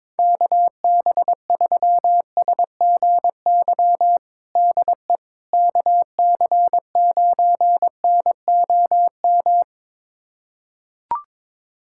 Morse code, heard as a keyed tone, about 22 wpm, 700 Hz. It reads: KB3SGY DE KC9NOM
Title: USA Callsign DE Callsign - 6 Characters - 22wpm